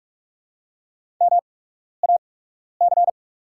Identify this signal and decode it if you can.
Morse code, heard as a keyed tone, about 48 wpm, spaced as for 12 wpm, 690 Hz. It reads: MAC
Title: 8x Character Spacing - Top 500 Words - 45wpm